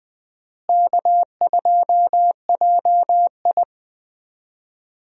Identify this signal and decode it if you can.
Morse code, heard as a keyed tone, about 21 wpm, 700 Hz. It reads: K2JI
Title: USA Call Signs - 4 Characters - Repeated 3x - 20wpm